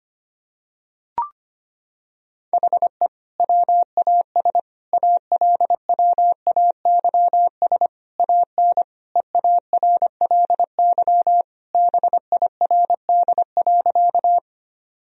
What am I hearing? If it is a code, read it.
Morse code, heard as a keyed tone, about 25 wpm, 700 Hz. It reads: HE WAS ALWAYS AN EARLY BIRD.
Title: Sentences from Top 1000 Words - Review Part 2 - 25wpm